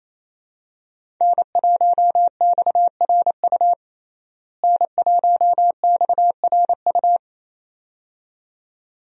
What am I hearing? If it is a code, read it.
Morse code, heard as a keyed tone, about 28 wpm, 700 Hz. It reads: N1XRU N1XRU
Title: USA Call Signs - 5 Characters - Repeated 2x - 28wpm